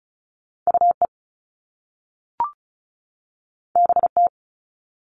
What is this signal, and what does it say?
Morse code, heard as a keyed tone, about 36 wpm, 690 Hz. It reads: UE BT
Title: All Cut Numbers - 2-Digit 35wpm